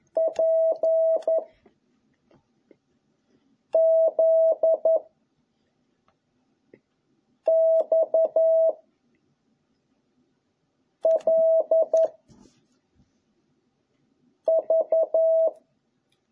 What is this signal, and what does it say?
Morse code, heard as a keyed tone, about 11 wpm, 660 Hz. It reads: P Z X L V